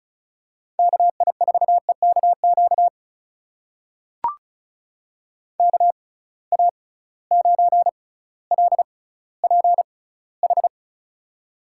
Morse code, heard as a keyed tone, about 36 wpm, spaced as for 18 wpm, 700 Hz. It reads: KI4EKQ KA9LPH
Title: USA Call Signs - 6 Characters - 6x Character Spacing - 35wpm